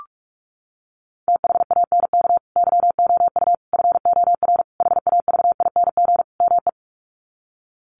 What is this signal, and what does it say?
Morse code, heard as a keyed tone, about 44 wpm, 700 Hz. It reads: THANK YOU FOR HAVING ME
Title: Sets of 5 Words Encore 45wpm